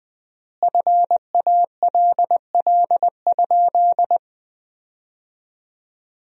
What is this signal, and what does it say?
Morse code, heard as a keyed tone, about 20 wpm, 700 Hz. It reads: FALL?
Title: 4x Word-Spacing - Sentences from Top 500 Words - 20wpm